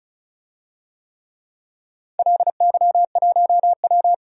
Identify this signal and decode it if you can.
Morse code, heard as a keyed tone, about 35 wpm, 700 Hz. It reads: LY1W
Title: DXCC-100 Call Signs - 4 Characters - 35wpm